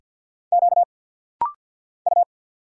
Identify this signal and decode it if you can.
Morse code, heard as a keyed tone, about 50 wpm, 700 Hz. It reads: Q U
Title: Single Letter-Number Alt - Rapid-Fire 50wpm